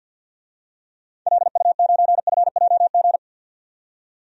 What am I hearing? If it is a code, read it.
Morse code, heard as a keyed tone, about 53 wpm, 700 Hz. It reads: LU9FJG
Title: DXCC-100 Call Signs - 6 Characters - 50wpm